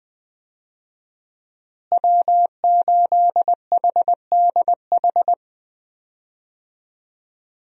Morse code, heard as a keyed tone, about 21 wpm, 700 Hz. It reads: W8HDH
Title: USA Call Signs - 5 Characters - Repeated 2x - 20wpm